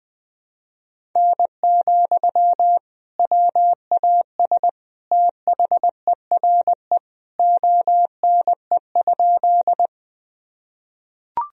Morse code, heard as a keyed tone, about 20 wpm, 700 Hz. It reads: N, WAS THERE ONE?
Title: Sentences from Top 100 Words and Abbreviations 20wpm